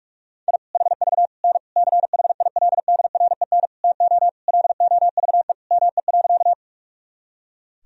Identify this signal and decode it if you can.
Morse code, heard as a keyed tone, about 46 wpm, 700 Hz. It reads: I HV N CHILDREN TO LOVE ME.
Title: Sentences from Top 400 Words and Abbreviations 45wpm